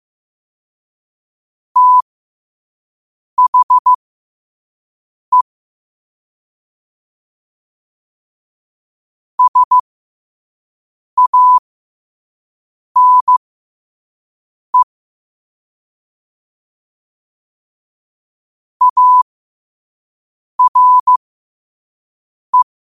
Morse code, heard as a keyed tone, about 15 wpm, spaced as for 5 wpm, 1000 Hz. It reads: THE SANE ARE